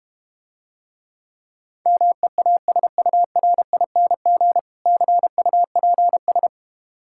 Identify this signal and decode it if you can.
Morse code, heard as a keyed tone, about 31 wpm, 700 Hz. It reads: MEASURING CUPS
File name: Easy Words - Speed-Racing with Hints - 25wpm